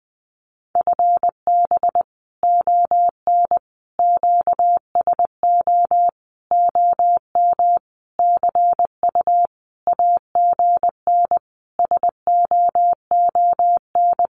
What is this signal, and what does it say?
Morse code, heard as a keyed tone, about 20 wpm, 700 Hz. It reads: FB ON QSO OM CU AGN SOON
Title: Common QSO Phrases 20wpm